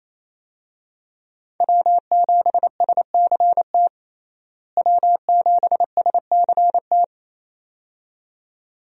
Morse code, heard as a keyed tone, about 28 wpm, 700 Hz. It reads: W7SCT W7SCT
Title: USA Call Signs - 5 Characters - Repeated 2x - 28wpm